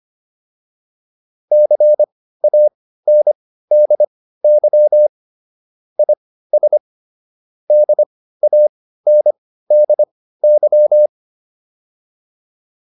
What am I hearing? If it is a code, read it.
Morse code, heard as a keyed tone, about 25 wpm, spaced as for 15 wpm, 600 Hz. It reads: CANDY IS DANDY